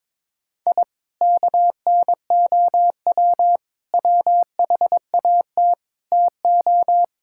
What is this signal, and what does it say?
Morse code, heard as a keyed tone, about 22 wpm, 700 Hz. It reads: I KNOW WHAT TO
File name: practice / 22wpm / Sentences with 3-Word Phrases - 22wpm